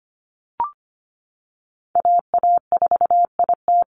Morse code, heard as a keyed tone, about 25 wpm, 700 Hz. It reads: AA4IT
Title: NA SOTA Chaser Callsigns 25wpm